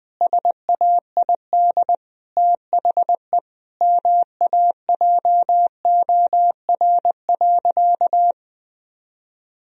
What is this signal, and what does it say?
Morse code, heard as a keyed tone, about 20 wpm, 700 Hz. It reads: SAID THE MAJOR.
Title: Sentences from Top 1000 Words - Review Part 4 - 20wpm